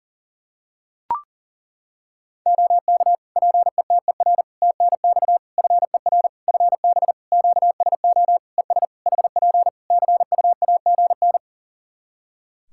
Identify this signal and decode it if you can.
Morse code, heard as a keyed tone, about 40 wpm, 700 Hz. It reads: OK PETER TNX FER FB QSO ES HP CUAGN
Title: Practice QSO - Exchange 4 - Ending - 40wpm